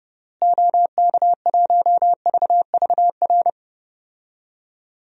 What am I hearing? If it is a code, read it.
Morse code, heard as a keyed tone, about 30 wpm, 700 Hz. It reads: OK1VVR
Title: DXCC-100 Call Signs - 6 Characters - Repeated 3x - 30wpm